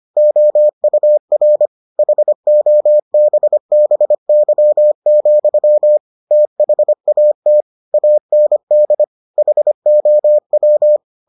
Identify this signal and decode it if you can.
Morse code, heard as a keyed tone, about 25 wpm, 600 Hz. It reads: OUR HOBBY, THAT AND HOW